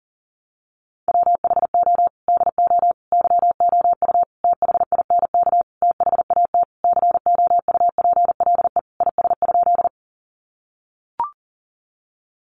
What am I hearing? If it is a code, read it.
Morse code, heard as a keyed tone, about 40 wpm, 700 Hz. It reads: WHO DO YOU THINK THAT COUPLE IS?
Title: Sentences from Top 1100 Words 40wpm